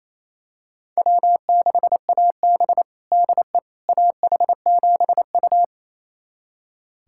Morse code, heard as a keyed tone, about 27 wpm, 700 Hz. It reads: W6AB DE AH7U
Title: USA Callsign DE Callsign - 4 Characters - 28wpm